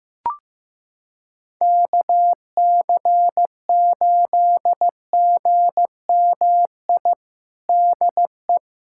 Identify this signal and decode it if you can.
Morse code, heard as a keyed tone, about 15 wpm, 700 Hz. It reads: KC8GMI DE
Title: USA Callsign DE Callsign - 6 Characters - 15wpm